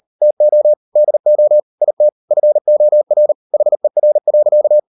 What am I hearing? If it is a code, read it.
Morse code, heard as a keyed tone, about 40 wpm, 600 Hz. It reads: TO DO IT FOR HER.